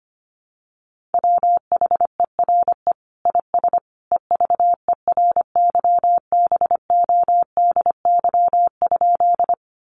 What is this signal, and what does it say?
Morse code, heard as a keyed tone, about 25 wpm, 700 Hz. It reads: WHERE IS EVERYBODY?
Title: Sentences from Top 1400 Words 25wpm